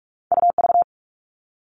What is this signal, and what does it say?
Morse code, heard as a keyed tone, about 46 wpm, 700 Hz. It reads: UV
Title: All Cut Numbers - 2-Digit 45wpm